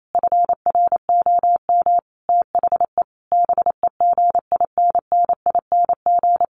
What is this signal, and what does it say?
Morse code, heard as a keyed tone, about 28 wpm, 700 Hz. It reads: FROM THE BEGINNING
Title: Sentences from Top 1300 Words 28wpm